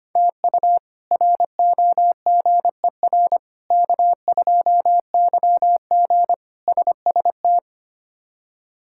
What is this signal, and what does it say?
Morse code, heard as a keyed tone, about 25 wpm, 700 Hz. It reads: TU ROGER K2YG SST
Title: K1USN SST QSO Practice - 25wpm